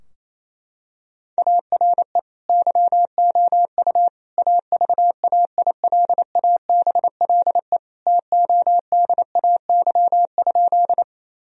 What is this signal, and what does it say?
Morse code, heard as a keyed tone, about 28 wpm, 700 Hz. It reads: ARE YOU AVAILABLE TODAY?
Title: Sentences from Top 1000 Words - Review Part 3 - 28wpm